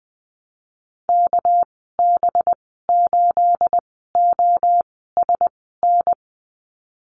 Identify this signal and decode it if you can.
Morse code, heard as a keyed tone, about 20 wpm, spaced as for 15 wpm, 700 Hz. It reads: KB8OSN
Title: 219 Lesson 35g - Number 0 - Call Signs 20wpm